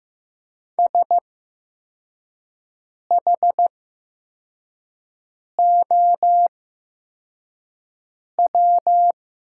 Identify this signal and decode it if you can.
Morse code, heard as a keyed tone, about 15 wpm, spaced as for 4 wpm, 700 Hz. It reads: SHOW
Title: 8x Character Spacing - Top 300 Words - 15wpm